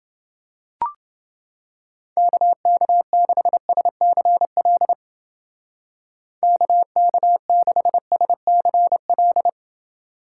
Morse code, heard as a keyed tone, about 29 wpm, 700 Hz. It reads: KK6SCL KK6SCL